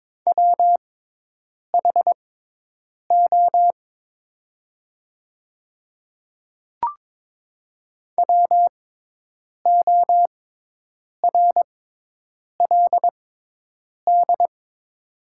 Morse code, heard as a keyed tone, about 22 wpm, spaced as for 8 wpm, 700 Hz. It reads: WHO WORLD
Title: 6x Character Spacing - Top 200 Words - 22wpm